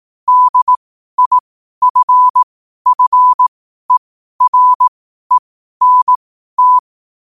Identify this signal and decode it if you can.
Morse code, heard as a keyed tone, about 18 wpm, spaced as for 12 wpm, 1000 Hz. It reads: DIFFERENT
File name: qotdfiles / 12wpm